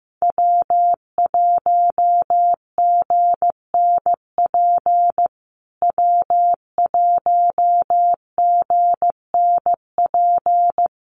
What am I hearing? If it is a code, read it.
Morse code, heard as a keyed tone, about 15 wpm, 700 Hz. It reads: W1GNP W1GNP
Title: USA Call Signs Repeated 3x 15wpm